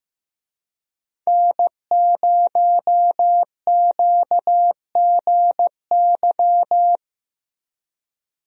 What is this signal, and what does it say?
Morse code, heard as a keyed tone, about 15 wpm, 700 Hz. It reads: N0QGY
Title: USA Call Signs - 5 Characters - Repeated 2x - 15wpm